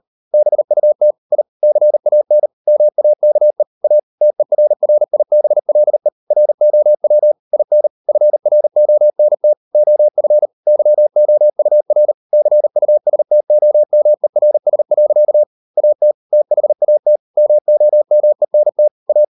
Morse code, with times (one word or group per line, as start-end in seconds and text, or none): 0.3-1.1 BUT
1.3-1.4 I
1.6-2.5 CAN
2.7-3.6 MAKE
3.8-4.0 A
4.2-6.1 TERRIBLE
6.3-7.3 ROW
7.5-7.9 IN
8.1-9.5 FRONT
9.8-10.5 OF
10.7-12.1 YOUR
12.3-15.4 CUSTOMERS.
15.8-16.1 AT
16.3-17.2 THAT
17.4-18.9 MOMENT
19.1-19.2 A